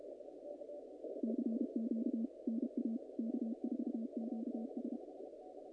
Morse code, heard as a keyed tone, about 31 wpm, 220 Hz. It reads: CQ NA K4QS
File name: contests / naqp-cw-k5tr-op-k5pi- -2100